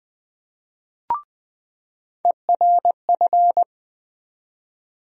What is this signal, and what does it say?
Morse code, heard as a keyed tone, about 20 wpm, 700 Hz. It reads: ERF
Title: USA Call Sign Postfixes - 3 Letters 20wpm